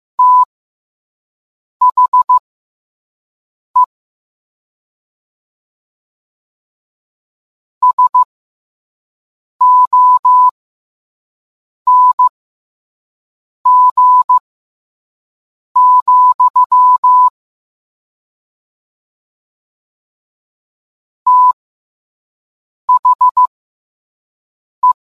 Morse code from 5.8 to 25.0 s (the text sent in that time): SONG, THE